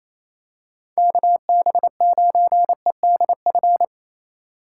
Morse code, heard as a keyed tone, about 28 wpm, 700 Hz. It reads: KB9EDF
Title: USA Call Signs - 6 Characters - 6x Character Spacing - 28wpm